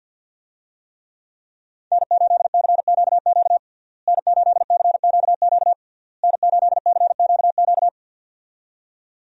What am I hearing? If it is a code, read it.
Morse code, heard as a keyed tone, about 51 wpm, 700 Hz. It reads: N8CQQ N8CQQ N8CQQ
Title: USA Call Signs - 5 Characters - Repeated 3x - 50wpm